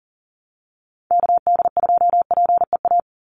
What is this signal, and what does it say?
Morse code, heard as a keyed tone, about 40 wpm, 700 Hz. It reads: KD2PEA
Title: USA Call Signs 40wpm